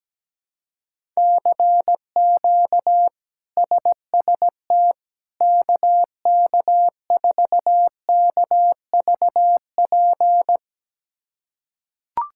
Morse code, heard as a keyed tone, about 17 wpm, 700 Hz. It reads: CQ SST KK4KVP
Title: K1USN SST QSO Practice - 17wpm